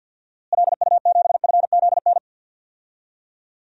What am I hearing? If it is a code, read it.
Morse code, heard as a keyed tone, about 52 wpm, 700 Hz. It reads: LU7FZN
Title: DXCC-100 Call Signs - 6 Characters - Repeated 2x - 50wpm